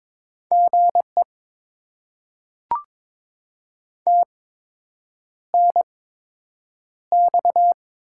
Morse code, heard as a keyed tone, about 22 wpm, spaced as for 8 wpm, 700 Hz. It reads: GE TNX
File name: Top 20 QSO Elements - 8x Character Spacing - 22wpm